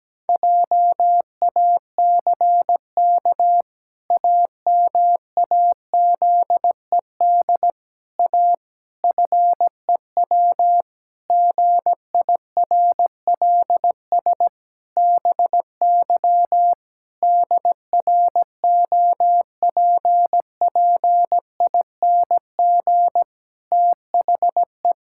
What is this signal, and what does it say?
Morse code, heard as a keyed tone, about 17 wpm, 700 Hz. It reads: JACK AMAZED A FEW GIRLS BY DROPPING THE
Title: Pangram Challenge v2 - 17wpm